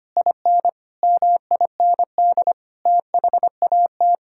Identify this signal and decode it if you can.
Morse code, heard as a keyed tone, about 25 wpm, 700 Hz. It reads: IN MIND THAT
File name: Sets of 9 Words - 25wpm